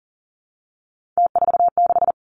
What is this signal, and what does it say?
Morse code, heard as a keyed tone, about 38 wpm, 700 Hz. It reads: T46
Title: Three Challenging Letter-Number Combinations 40wpm